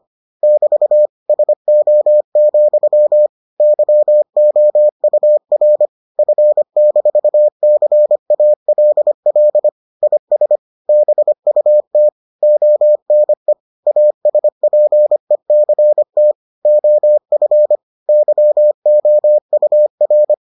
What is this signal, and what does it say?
Morse code, heard as a keyed tone, about 25 wpm, 600 Hz. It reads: <BT> SO, YOUR F-CALL IS BUT ONE ASPECT OF YOUR